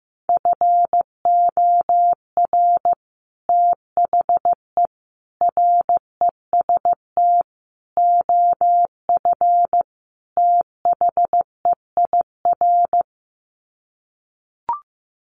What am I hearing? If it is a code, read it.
Morse code, heard as a keyed tone, about 15 wpm, 700 Hz. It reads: FOR THE REST OF THEIR
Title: Sets of 5 Words Encore2 15wpm